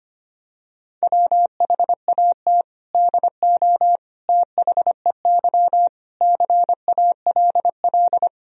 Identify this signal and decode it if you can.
Morse code, heard as a keyed tone, about 25 wpm, 700 Hz. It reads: WHAT DO THEY CALL
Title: Sentences from Top 100 Words and Abbreviations 25wpm